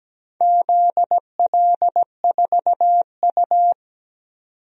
Morse code, heard as a keyed tone, about 17 wpm, 700 Hz. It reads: ZL4U
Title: DXCC-100 Call Signs - 4 Characters - 17wpm